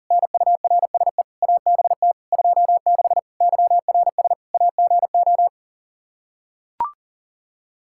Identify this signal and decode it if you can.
Morse code, heard as a keyed tone, about 37 wpm, 700 Hz. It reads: NURSE ABT 26 YRS AGO
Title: Practice QSO - Exchange 3 - More Info - 40wpm